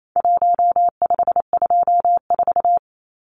Morse code, heard as a keyed tone, about 28 wpm, 700 Hz. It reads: 1524
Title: Numbers - 4-Digit 28wpm